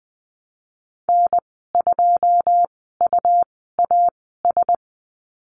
Morse code, heard as a keyed tone, about 19 wpm, spaced as for 14 wpm, 700 Hz. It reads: N2UAS
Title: 122 Lesson 20g - Number 3 - Call Signs 20wpm